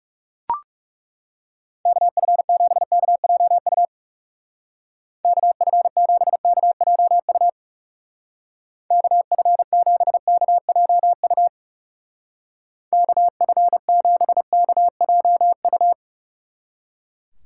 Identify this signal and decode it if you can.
Morse code, heard as a keyed tone, about 36 wpm, 700 Hz. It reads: KF7KJU KF7KJU KF7KJU KF7KJU
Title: Speed Racing - USA Call Signs 30wpm